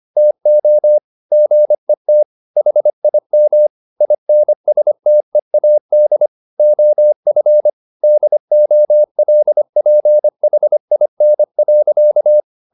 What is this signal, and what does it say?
Morse code, heard as a keyed tone, about 25 wpm, 600 Hz. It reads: TO GET HIM INSTEAD OF DOLPHIN.
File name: the old man and the sea part 1 25wpm